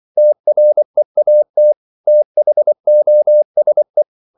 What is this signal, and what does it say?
Morse code, heard as a keyed tone, about 24 wpm, 600 Hz. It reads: TREAT THOSE
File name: CW / Practice / If24